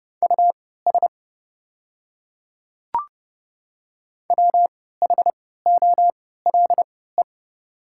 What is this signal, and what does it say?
Morse code, heard as a keyed tone, about 31 wpm, spaced as for 17 wpm, 690 Hz. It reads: US WHOLE
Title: 3x Character Spacing - Top 500 Words - 30wpm